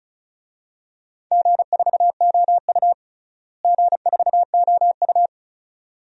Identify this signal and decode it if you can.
Morse code, heard as a keyed tone, about 36 wpm, 700 Hz. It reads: G4OU G4OU
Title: DXCC-100 Call Signs - 4 Characters - Repeated 2x - 35wpm